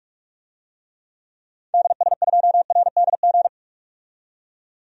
Morse code, heard as a keyed tone, about 46 wpm, 700 Hz. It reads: DS2RDG
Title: DXCC-100 Call Signs - 6 Characters - 45wpm